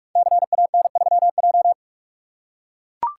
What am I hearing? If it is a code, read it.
Morse code, heard as a keyed tone, about 44 wpm, 700 Hz. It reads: CAN3J
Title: Code Groups - 5 Letter:Numbers - 45wpm